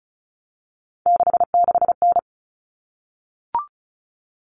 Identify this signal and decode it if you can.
Morse code, heard as a keyed tone, about 34 wpm, 700 Hz. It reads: 66N
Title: All Cut Numbers - 3-Digit 35wpm